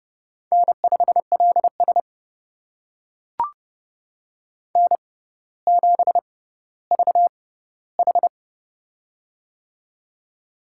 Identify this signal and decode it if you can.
Morse code, heard as a keyed tone, about 28 wpm, spaced as for 15 wpm, 700 Hz. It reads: N5LS N7VH
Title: USA Call Signs - 4 Characters - 6x Character Spacing - 30wpm